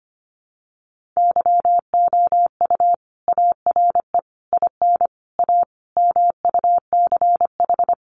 Morse code, heard as a keyed tone, about 25 wpm, 700 Hz. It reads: YOU ARE IN A MUCH
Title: Sentences from Top 1400 Words 25wpm